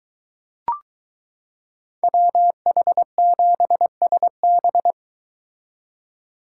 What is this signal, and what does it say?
Morse code, heard as a keyed tone, about 22 wpm, 700 Hz. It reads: WH7SB